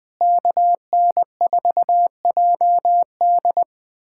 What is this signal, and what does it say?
Morse code, heard as a keyed tone, about 20 wpm, 700 Hz. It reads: KN4JD
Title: USA Callsign DE Callsign - 5 Characters - 20wpm